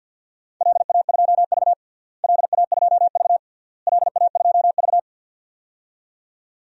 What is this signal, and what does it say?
Morse code, heard as a keyed tone, about 52 wpm, 700 Hz. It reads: LA2V LA2V LA2V
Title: DXCC-100 Call Signs - 4 Characters - Repeated 3x - 50wpm